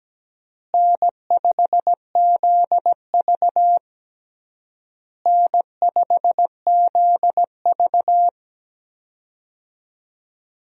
Morse code, heard as a keyed tone, about 17 wpm, 700 Hz. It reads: N5ZV N5ZV
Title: USA Call Signs - 4 Characters - Repeated 3x - 17wpm